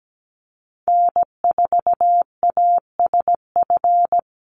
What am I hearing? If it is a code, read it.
Morse code, heard as a keyed tone, about 17 wpm, 700 Hz. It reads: N4ASF
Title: Call Signs Repeated 3x - 17wpm